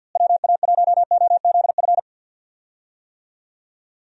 Morse code, heard as a keyed tone, about 53 wpm, 700 Hz. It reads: WA1OZF
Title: USA Call Signs - 6 Characters - Repeated 3x - 50wpm